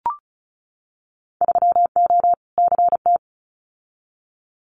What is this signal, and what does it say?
Morse code, heard as a keyed tone, about 34 wpm, 690 Hz. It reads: 3O CT
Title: Winter Field Day Exchanges 35wpm